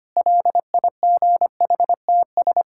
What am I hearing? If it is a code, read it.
Morse code, heard as a keyed tone, about 25 wpm, 700 Hz. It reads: LIGHTS
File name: practice / 25wpm / Difficult Sentences with Hints - 25wpm